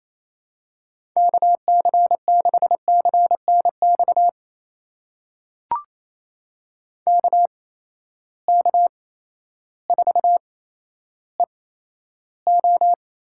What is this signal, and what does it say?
Morse code, heard as a keyed tone, about 28 wpm, spaced as for 13 wpm, 700 Hz. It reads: KC6CNX KK4EO
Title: USA Call Signs - 6 Characters - 8x Character Spacing - 28wpm